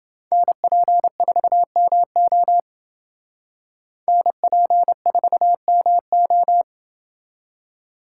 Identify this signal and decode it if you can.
Morse code, heard as a keyed tone, about 28 wpm, 700 Hz. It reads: NP4MO NP4MO